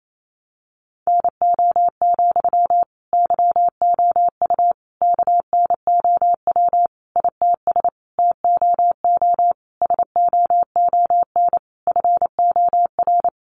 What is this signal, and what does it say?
Morse code, heard as a keyed tone, about 28 wpm, 700 Hz. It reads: NO, YOU KNOW ITS TOO SOON FOR
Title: Sentences from Top 800 Words 28wpm